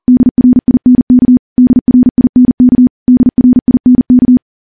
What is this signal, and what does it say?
Morse code, heard as a keyed tone, about 40 wpm, 250 Hz. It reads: DRINK DRINK DRINK
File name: morse / morse-training-40WPM